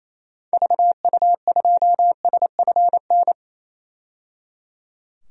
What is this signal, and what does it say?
Morse code, heard as a keyed tone, about 28 wpm, 700 Hz. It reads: VU2SFN
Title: DXCC-100 Call Signs - 6 Characters - 28wpm